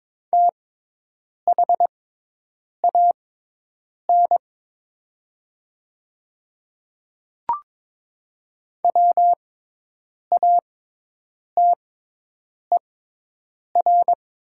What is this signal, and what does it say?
Morse code, heard as a keyed tone, about 22 wpm, spaced as for 8 wpm, 700 Hz. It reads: THAN WATER
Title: 6x Character Spacing - Top 100 Words - 22wpm